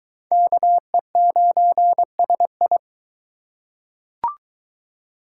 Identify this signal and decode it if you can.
Morse code, heard as a keyed tone, about 22 wpm, 700 Hz. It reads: KE9SI